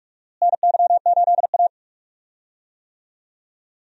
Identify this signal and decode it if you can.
Morse code, heard as a keyed tone, about 46 wpm, 700 Hz. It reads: NY8A
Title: USA Call Signs - 4 Characters - Repeated 3x - 45wpm